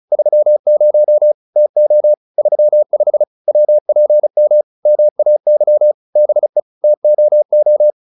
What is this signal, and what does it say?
Morse code, heard as a keyed tone, about 35 wpm, 600 Hz. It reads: 30 TO 35 WPM MAY BE TOO